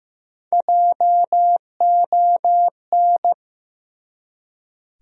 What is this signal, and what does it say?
Morse code, heard as a keyed tone, about 15 wpm, 700 Hz. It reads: JON
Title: USA Common Names - Max 3 Letters - 15wpm